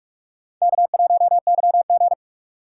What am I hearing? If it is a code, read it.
Morse code, heard as a keyed tone, about 58 wpm, 700 Hz. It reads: K1YG